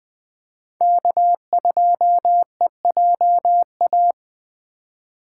Morse code, heard as a keyed tone, about 21 wpm, 700 Hz. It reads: K2EJA
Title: USA Call Signs - 5 Characters - 3x Character Spacing - 20wpm